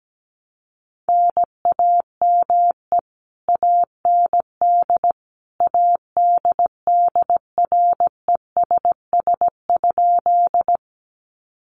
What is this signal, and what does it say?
Morse code, heard as a keyed tone, about 17 wpm, 700 Hz. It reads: NAME AND ADDRESS?
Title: Sentences from Top 1000 Words 17wpm